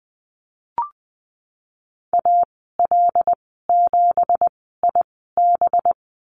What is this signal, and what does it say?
Morse code, heard as a keyed tone, about 20 wpm, spaced as for 15 wpm, 700 Hz. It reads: AL7IB
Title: 200 Lesson 32g - Letter k - Call Signs 20wpm